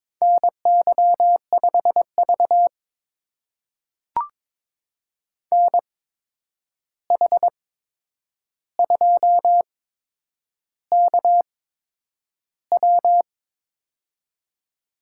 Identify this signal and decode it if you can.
Morse code, heard as a keyed tone, about 22 wpm, spaced as for 9 wpm, 700 Hz. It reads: NY5V NH2KW
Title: USA Call Signs - 8x Character Spacing - 22wpm